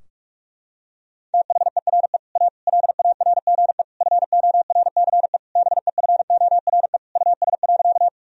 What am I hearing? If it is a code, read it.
Morse code, heard as a keyed tone, about 46 wpm, 700 Hz. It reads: THERE A LARGE FORCE BEFORE US.
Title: Sentences from Top 1000 Words - Review Part 3 - 45wpm